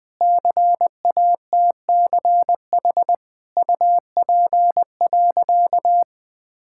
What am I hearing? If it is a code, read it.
Morse code, heard as a keyed tone, about 20 wpm, 700 Hz. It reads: CATCH UP.
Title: Difficult Sentences with Hints - 20wpm